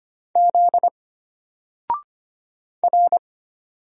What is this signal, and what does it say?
Morse code, heard as a keyed tone, about 25 wpm, 700 Hz. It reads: Z R
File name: Single-Letter-Number-25wpm